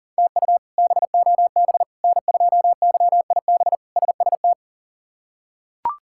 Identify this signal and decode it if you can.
Morse code, heard as a keyed tone, about 40 wpm, 700 Hz. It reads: TU BOB N2YIB SST
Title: K1USN SST QSO Practice - 40wpm